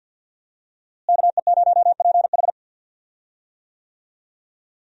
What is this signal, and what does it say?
Morse code, heard as a keyed tone, about 46 wpm, 700 Hz. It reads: KE0PH
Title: USA Call Signs - 5 Characters - Repeated 3x - 50wpm